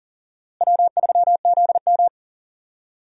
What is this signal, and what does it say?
Morse code, heard as a keyed tone, about 40 wpm, 700 Hz. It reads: W3ZM
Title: USA Call Signs - 4 Characters - Repeated 3x - 40wpm